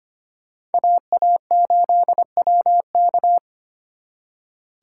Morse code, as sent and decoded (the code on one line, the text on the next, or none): .- .- ---.. .-- -.-
AA8WK